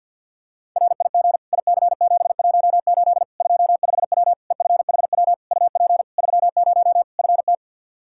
Morse code, heard as a keyed tone, about 47 wpm, 700 Hz. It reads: RIG IC718 25W EFHW UP 30 FT
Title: Practice QSO - Exchange 2 - Info - 50wpm